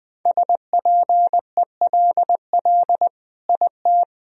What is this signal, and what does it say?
Morse code, heard as a keyed tone, about 20 wpm, 700 Hz. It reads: SPELL IT
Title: Sentences from Top 1000 Words - Review Part 1 - 20wpm